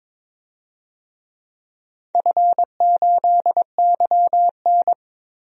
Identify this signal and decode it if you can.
Morse code, heard as a keyed tone, about 22 wpm, 700 Hz. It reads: F8YN
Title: DXCC-100 Call Signs - 4 Characters - 22wpm